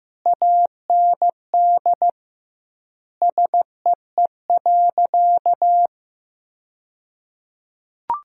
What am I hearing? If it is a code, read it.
Morse code, heard as a keyed tone, about 15 wpm, 700 Hz. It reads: AND SEE.
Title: 2x Word-Spacing - Sentences from Top 100 Words - 15wpm